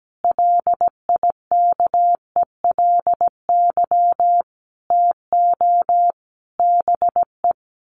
Morse code, heard as a keyed tone, about 17 wpm, 700 Hz. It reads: LIKELY TO BE
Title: Sets of 5 Words Encore2 17wpm